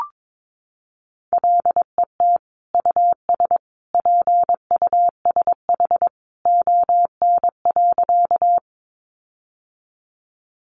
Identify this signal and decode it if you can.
Morse code, heard as a keyed tone, about 22 wpm, 700 Hz. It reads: LET US PUSH ON.
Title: Sentences from Top 1300 Words 22wpm